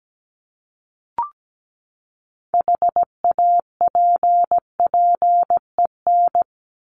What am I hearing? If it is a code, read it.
Morse code, heard as a keyed tone, about 17 wpm, 700 Hz. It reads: HAPPEN
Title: ED Suffix - Top 100 Words with Base Words 17wpm